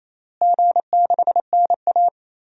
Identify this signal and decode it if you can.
Morse code, heard as a keyed tone, about 28 wpm, 700 Hz. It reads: G6NA
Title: DXCC-100 Call Signs - 4 Characters - Repeated 2x - 28wpm